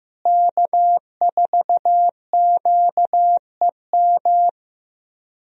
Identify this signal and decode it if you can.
Morse code, heard as a keyed tone, about 16 wpm, 700 Hz. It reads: K4QEM
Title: USA Call Signs - 5 Characters - Repeated 2x - 15wpm